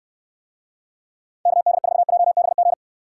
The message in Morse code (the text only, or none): KD4JBG